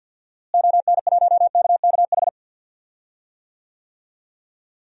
Morse code, heard as a keyed tone, about 53 wpm, 700 Hz. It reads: ON1KKH
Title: DXCC-100 Call Signs - 6 Characters - Repeated 2x - 50wpm